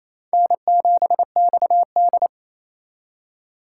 Morse code, heard as a keyed tone, about 28 wpm, 700 Hz. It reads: N7XD
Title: USA Call Signs - 4 Characters - 3x Character Spacing - 28wpm